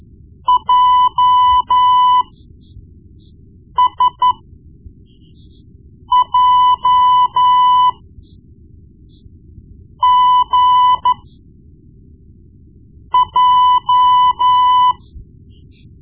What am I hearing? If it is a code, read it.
Morse code, heard as a keyed tone, about 12 wpm, spaced as for 4 wpm, 1000 Hz. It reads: JSJGJ
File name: Morse Lessons / Morse Lessons / Tape-1B